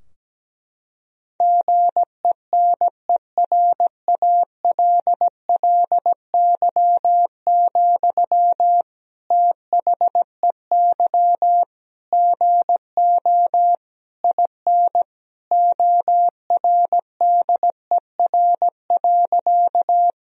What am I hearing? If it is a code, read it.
Morse code, heard as a keyed tone, about 17 wpm, 700 Hz. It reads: GENERALLY, THEY GO IN ORDER.
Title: Sentences from Top 1000 Words - Review Part 4 - 17wpm